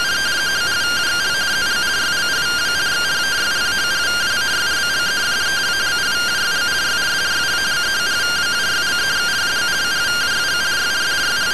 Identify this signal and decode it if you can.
Two-tone teleprinter signal, about 170 Hz shift.